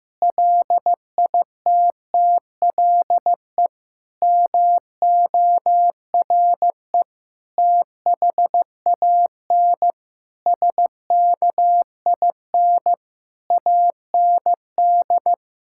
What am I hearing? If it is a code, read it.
Morse code, heard as a keyed tone, about 15 wpm, 700 Hz. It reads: LITTLE MORE THAN SKIN AND
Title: Sentences from Top 1600 Words 15wpm